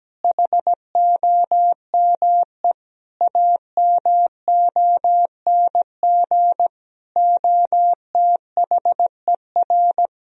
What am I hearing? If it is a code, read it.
Morse code, heard as a keyed tone, about 17 wpm, 700 Hz. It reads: HOME AMONG OTHER
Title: Sentences from Top 1000 Words - Review Part 3 - 17wpm